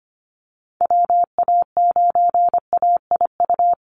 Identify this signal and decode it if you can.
Morse code, heard as a keyed tone, about 25 wpm, 700 Hz. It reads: WA9AIU
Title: USA Call Signs 25wpm